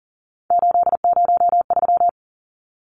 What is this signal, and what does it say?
Morse code, heard as a keyed tone, about 40 wpm, 700 Hz. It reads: 803
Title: Numbers - 3-Digit 40wpm